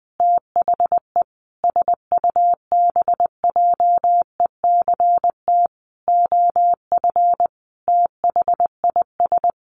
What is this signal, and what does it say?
Morse code, heard as a keyed tone, about 20 wpm, 700 Hz. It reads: THE SUBJECT OF THIS
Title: Sentences from Top 1100 Words 20wpm